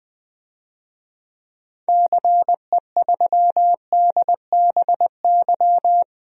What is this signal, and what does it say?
Morse code, heard as a keyed tone, about 20 wpm, 700 Hz. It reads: CE3DBY
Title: DXCC-100 Call Signs - 6 Characters - 20wpm